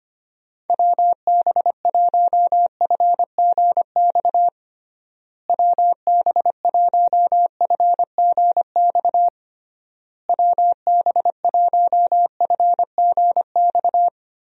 Morse code, heard as a keyed tone, about 25 wpm, 700 Hz. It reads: WB1FGX WB1FGX WB1FGX
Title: USA Call Signs - 6 Characters - Repeated 3x - 25wpm